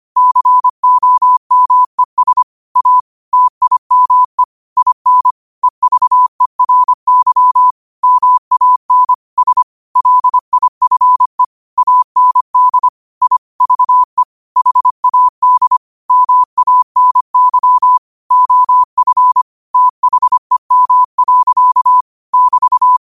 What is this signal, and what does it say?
Morse code, heard as a keyed tone, about 25 wpm, 1000 Hz. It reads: COMES A TIME IN EVERY MAN S LIFE AND I VE HAD MANY OF THEM. <BT>
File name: qotdfiles / 25wpm